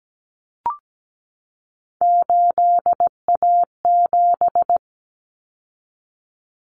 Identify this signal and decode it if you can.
Morse code, heard as a keyed tone, about 17 wpm, 700 Hz. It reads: 8A7
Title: Common Cut Numbers - 3-Digit 17wpm